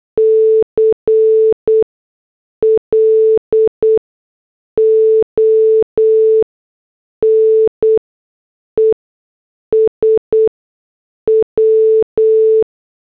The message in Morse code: CLONES W